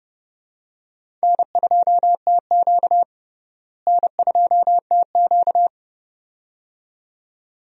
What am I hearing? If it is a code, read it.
Morse code, heard as a keyed tone, about 30 wpm, 700 Hz. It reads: N2TQ N2TQ
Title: USA Call Signs - 4 Characters - Repeated 2x - 30wpm